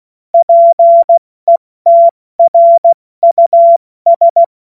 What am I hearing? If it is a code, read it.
Morse code, heard as a keyed tone, about 16 wpm, spaced as for 14 wpm, 680 Hz. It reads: PETRUS
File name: Morsekurs / Witze / Witz 02. Februar 2021, 70 F80